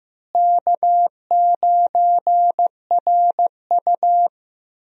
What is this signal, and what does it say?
Morse code, heard as a keyed tone, about 15 wpm, 700 Hz. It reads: K9RU
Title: USA Call Signs - 4 Characters - Repeated 2x - 15wpm